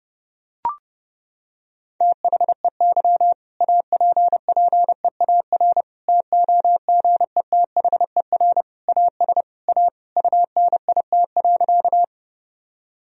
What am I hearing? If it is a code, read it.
Morse code, heard as a keyed tone, about 30 wpm, 700 Hz. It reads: THEY APPEAR TOGETHER AS A UNIT.
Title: Sentences from Top 1000 Words - Review Part 2 - 30wpm